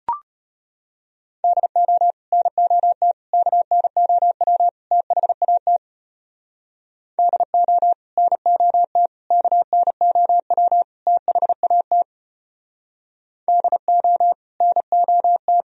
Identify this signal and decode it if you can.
Morse code, heard as a keyed tone, about 36 wpm, 700 Hz. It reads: DO NOT KNOW THAT DO NOT KNOW THAT DO NOT